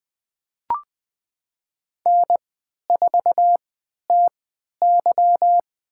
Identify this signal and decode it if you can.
Morse code, heard as a keyed tone, about 20 wpm, spaced as for 11 wpm, 700 Hz. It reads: N4TY
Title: USA Call Signs - 3x Character Spacing - 20wpm